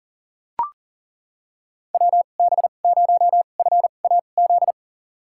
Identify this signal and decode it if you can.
Morse code, heard as a keyed tone, about 37 wpm, spaced as for 28 wpm, 700 Hz. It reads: WB0FAZ
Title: USA Call Signs - 2x Character Spacing - 40wpm